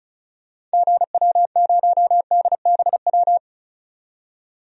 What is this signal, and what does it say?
Morse code, heard as a keyed tone, about 34 wpm, 700 Hz. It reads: GW0DBW
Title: DXCC-100 Call Signs - 6 Characters - 35wpm